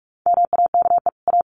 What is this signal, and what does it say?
Morse code, heard as a keyed tone, about 44 wpm, 700 Hz. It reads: MAKE A
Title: Sets of 2 Words 45wpm